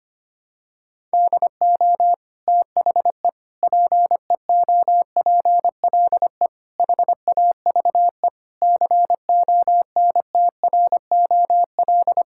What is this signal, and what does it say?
Morse code, heard as a keyed tone, about 25 wpm, 700 Hz. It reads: DO THE PEOPLE HAVE CONTROL
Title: Sentences from Top 1000 Words - Review Part 5 - 25wpm